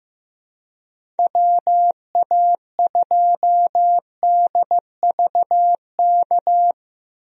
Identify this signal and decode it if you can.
Morse code, heard as a keyed tone, about 15 wpm, 700 Hz. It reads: WA2DVK
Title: USA Call Signs - 6 Characters - 15wpm